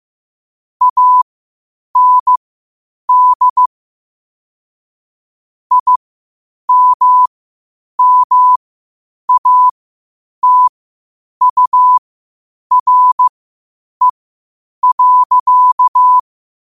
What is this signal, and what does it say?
Morse code, heard as a keyed tone, about 15 wpm, spaced as for 8 wpm, 1000 Hz. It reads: AND IMMATURE.
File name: qotdfiles / 8wpm